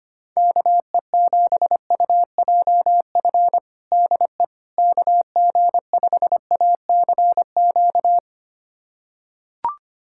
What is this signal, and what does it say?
Morse code, heard as a keyed tone, about 25 wpm, 700 Hz. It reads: KE7UJF DE KG5ACQ
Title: USA Callsign DE Callsign - 6 Characters - 25wpm